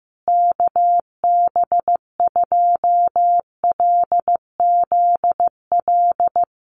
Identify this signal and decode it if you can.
Morse code, heard as a keyed tone, about 15 wpm, 700 Hz. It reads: KB2LZL
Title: USA Call Signs Repeated 3x 15wpm